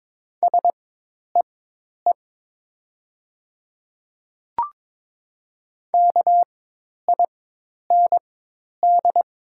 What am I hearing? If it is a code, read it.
Morse code, heard as a keyed tone, about 22 wpm, spaced as for 10 wpm, 700 Hz. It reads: SEE KIND
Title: 4x Character Spacing - Top 500 Words - 22wpm